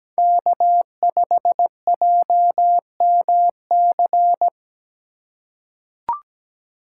Morse code, heard as a keyed tone, about 17 wpm, 700 Hz. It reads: K5JMC
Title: USA Call Signs - 5 Characters - 2x Character Spacing - 17wpm